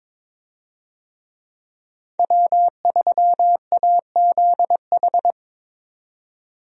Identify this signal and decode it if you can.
Morse code, heard as a keyed tone, about 22 wpm, 700 Hz. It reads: W3AZH
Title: USA Call Signs - 5 Characters - 22wpm